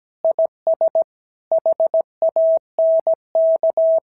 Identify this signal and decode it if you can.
Morse code, heard as a keyed tone, about 17 wpm, 660 Hz. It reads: IS HANK
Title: SOTA QSO Practice 17wpm